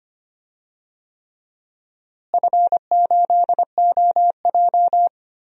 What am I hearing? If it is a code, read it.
Morse code, heard as a keyed tone, about 25 wpm, 700 Hz. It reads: F8OJ
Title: DXCC-100 Call Signs - 4 Characters - Repeated 3x - 25wpm